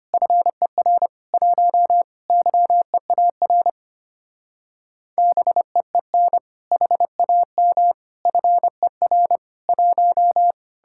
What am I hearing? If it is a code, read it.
Morse code, heard as a keyed tone, about 31 wpm, 700 Hz. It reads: FER 1 YEAR BEEN HAM FER 1